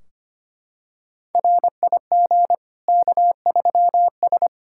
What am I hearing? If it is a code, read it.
Morse code, heard as a keyed tone, about 25 wpm, 700 Hz. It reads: RIG K3S
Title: K1USN SST QSO Practice 25wpm